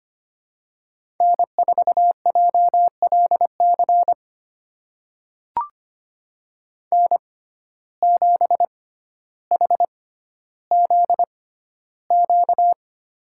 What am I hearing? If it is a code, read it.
Morse code, heard as a keyed tone, about 25 wpm, spaced as for 13 wpm, 700 Hz. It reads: N4JLC N7HZQ
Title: USA Call Signs - 5 Characters - 6x Character Spacing - 25wpm